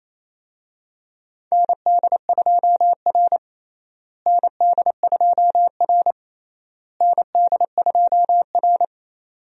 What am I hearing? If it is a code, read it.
Morse code, heard as a keyed tone, about 28 wpm, 700 Hz. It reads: ND2R ND2R ND2R
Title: USA Call Signs - 4 Characters - Repeated 3x - 28wpm